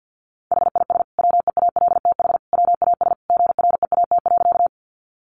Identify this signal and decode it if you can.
Morse code, heard as a keyed tone, about 50 wpm, 700 Hz. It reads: HIS WEALTH WAS GREAT.
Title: Sentences from Top 1200 Words 50wpm